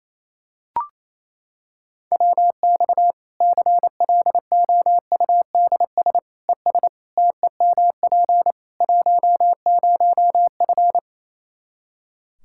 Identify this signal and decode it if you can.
Morse code, heard as a keyed tone, about 28 wpm, 700 Hz. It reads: WX CLOUDS ES TEMP 10F
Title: Practice QSO - Exchange 2 - Info - 28wpm